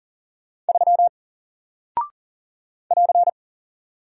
Morse code, heard as a keyed tone, about 40 wpm, 700 Hz. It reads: 3 <AR>
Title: Single Character (Letter:Number:Punctuation:Prosign) Alt 40wpm